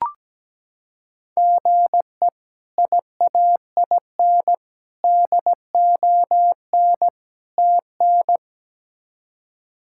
Morse code, heard as a keyed tone, about 17 wpm, 700 Hz. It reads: GE IAIN DON TN
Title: K1USN SST QSO Practice - 17wpm